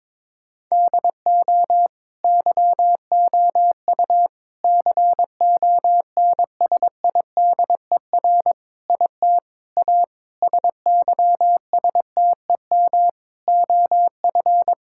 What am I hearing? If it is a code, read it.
Morse code, heard as a keyed tone, about 22 wpm, 700 Hz. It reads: DO YOU CONSIDER IT A SYSTEM OF
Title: Sentences from Top 1000 Words - Review Part 6 - 22wpm